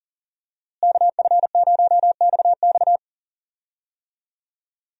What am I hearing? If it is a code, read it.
Morse code, heard as a keyed tone, about 40 wpm, 700 Hz. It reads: KF0XX
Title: USA Call Signs - 5 Characters - Repeated 2x - 40wpm